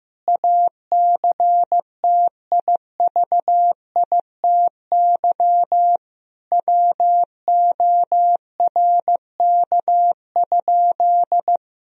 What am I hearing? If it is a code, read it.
Morse code, heard as a keyed tone, about 15 wpm, 700 Hz. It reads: ACTIVITY WORK?
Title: Sentences from Top 1000 Words - Review Part 4 - 15wpm